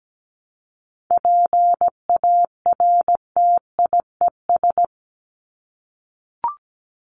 Morse code, heard as a keyed tone, about 17 wpm, 700 Hz. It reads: PARTIES
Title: ES:S Suffix - Top 100 Words with Base Words 17wpm